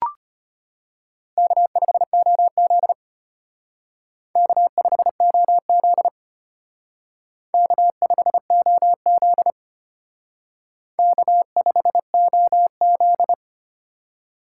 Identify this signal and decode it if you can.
Morse code, heard as a keyed tone, about 32 wpm, 700 Hz. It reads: K5OZ K5OZ K5OZ K5OZ